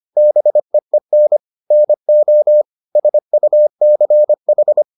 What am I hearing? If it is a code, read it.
Morse code, heard as a keyed tone, about 25 wpm, 600 Hz. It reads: BEEN NO SUCH